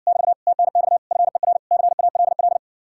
Morse code, heard as a keyed tone, about 66 wpm, 700 Hz. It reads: <BT> TNX FER CALL